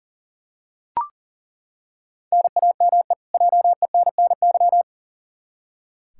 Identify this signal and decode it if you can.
Morse code, heard as a keyed tone, about 40 wpm, 700 Hz. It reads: NAME JENNY
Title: K1USN SST QSO Practice 40wpm